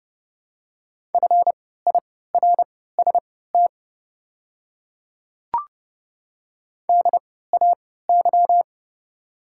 Morse code, heard as a keyed tone, about 29 wpm, spaced as for 17 wpm, 690 Hz. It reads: FIRST DAY
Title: 3x Character Spacing - Top 100 Words - 30wpm